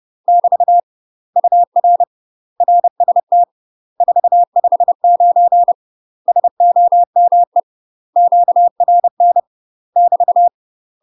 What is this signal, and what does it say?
Morse code, heard as a keyed tone, about 30 wpm, 700 Hz. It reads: <BT> UR RST 459 SOME QRN <BT>
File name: UK MASTER QSOs TRAINING/30WPM